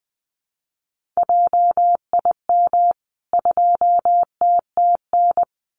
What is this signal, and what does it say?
Morse code, heard as a keyed tone, about 20 wpm, 700 Hz. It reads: JIM 2TTN
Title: CWT Exchange 20wpm